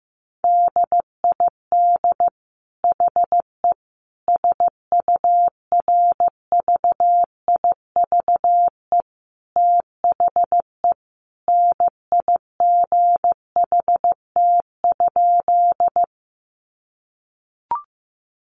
Morse code, heard as a keyed tone, about 15 wpm, 700 Hz. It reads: DID HE SURVIVE THE NIGHT?
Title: Sentences from Top 1300 Words 15wpm